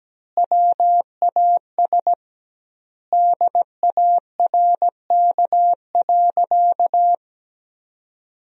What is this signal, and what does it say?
Morse code, heard as a keyed tone, about 17 wpm, 700 Hz. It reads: WAS DARK.
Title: 2x Word-Spacing - Sentences from Top 500 Words - 17wpm